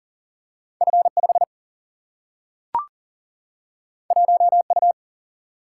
Morse code, heard as a keyed tone, about 40 wpm, 700 Hz. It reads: F 5 1 U
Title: Single-Letter Single-Number 40wpm